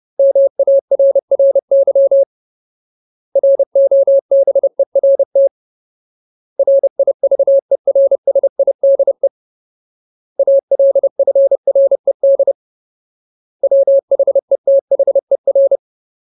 Morse code, heard as a keyed tone, about 30 wpm, 550 Hz. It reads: MARRY ROBERT RIVERSIDE ALFRED WHETHER
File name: Ultimate Word List / 30WPM Ultimate